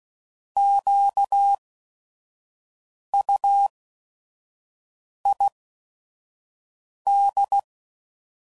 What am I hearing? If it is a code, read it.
Morse code, heard as a keyed tone, about 16 wpm, spaced as for 5 wpm, 800 Hz. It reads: QUID
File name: Prefixes / By First Letter / Qs